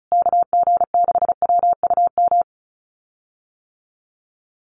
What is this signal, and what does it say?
Morse code, heard as a keyed tone, about 34 wpm, 700 Hz. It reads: KG6WUM
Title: USA Call Signs Repeated 3x 35wpm